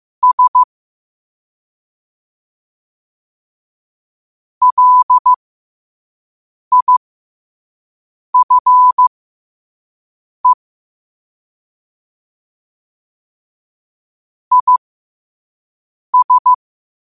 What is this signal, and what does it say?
Morse code, heard as a keyed tone, about 15 wpm, spaced as for 5 wpm, 1000 Hz. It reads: S LIFE IS
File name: qotdfiles / 5wpm